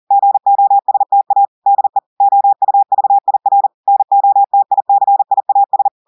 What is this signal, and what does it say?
Morse code, heard as a keyed tone, about 40 wpm, 800 Hz. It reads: GOSTA DE OUVIR NOTICIAS
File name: audios / portugais / portugais texte en clair 2 vitesse 40 mots minutes